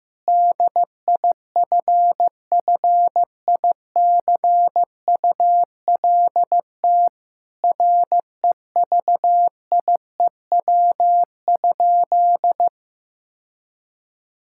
Morse code, heard as a keyed tone, about 15 wpm, 700 Hz. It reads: DIFFICULT REVIEW?
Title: Sentences from Top 1000 Words - Review Part 6 - 15wpm